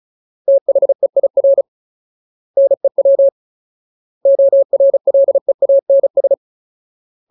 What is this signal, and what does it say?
Morse code, heard as a keyed tone, about 34 wpm, 550 Hz. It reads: THEIR NEW ORLEANS
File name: Ultimate Word List / 35WPM Ultimate